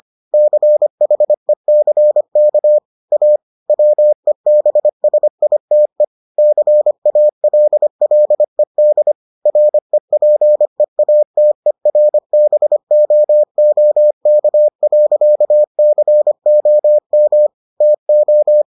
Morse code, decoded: CHECK A WEBSITE CALLED REPEATERBOOK.COM TO

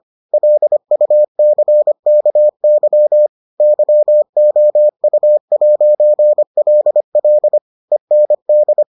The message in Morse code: LUCKY YOU'LL END

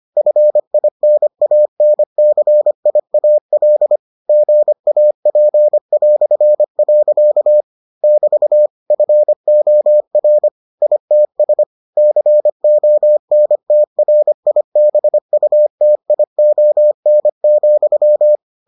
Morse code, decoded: FINANCIAL GAP". <BT> FOR ITS CONTRIBUTION,

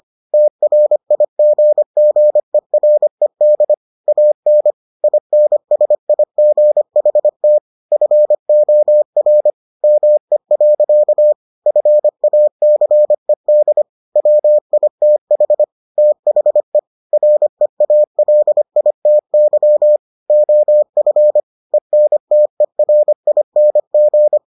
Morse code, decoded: TRIGGERED AN INSIGHT FOR ME. FACED WITH THE REALITY OF ENTERING

